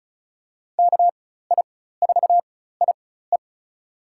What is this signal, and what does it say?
Morse code, heard as a keyed tone, about 36 wpm, spaced as for 16 wpm, 700 Hz. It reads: KI4IE
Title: USA Call Signs - 5 Characters - 4x Character Spacing - 35wpm